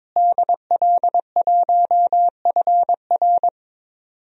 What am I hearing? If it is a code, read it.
Morse code, heard as a keyed tone, about 21 wpm, 700 Hz. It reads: DL1FR
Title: DXCC-100 Call Signs - 5 Characters - Repeated 2x - 22wpm